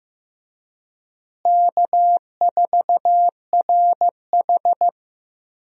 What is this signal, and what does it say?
Morse code, heard as a keyed tone, about 15 wpm, 700 Hz. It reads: K4RH
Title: USA Call Signs - 4 Characters - Repeated 2x - 15wpm